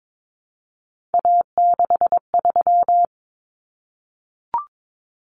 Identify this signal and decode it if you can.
Morse code, heard as a keyed tone, about 22 wpm, 700 Hz. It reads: A63
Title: Common Cut Numbers - 3-Digit 22wpm